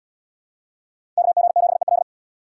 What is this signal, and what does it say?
Morse code, heard as a keyed tone, about 70 wpm, 700 Hz. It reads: KG7EL